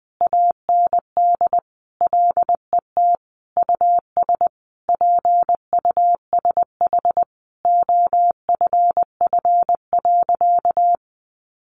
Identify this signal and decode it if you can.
Morse code, heard as a keyed tone, about 20 wpm, 700 Hz. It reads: AND LET US PUSH OFF.
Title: Sentences from Top 1300 Words 20wpm